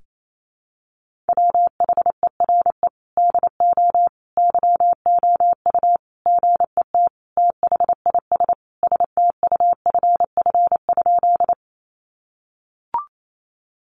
Morse code, heard as a keyed tone, about 28 wpm, 700 Hz. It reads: WHERE DO YOU GET THIS STUFF?
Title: Sentences from Top 1400 Words 28wpm